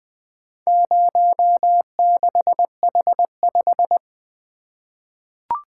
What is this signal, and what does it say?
Morse code, heard as a keyed tone, about 20 wpm, 700 Hz. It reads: 06H5
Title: Four Letter-Number Combinations - 20wpm